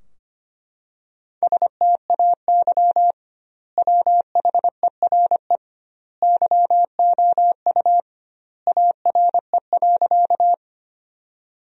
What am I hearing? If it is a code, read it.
Morse code, heard as a keyed tone, about 25 wpm, 700 Hz. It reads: STAY WHERE YOU ARE.
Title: 2x Word-Spacing - Sentences from Top 500 Words - 25wpm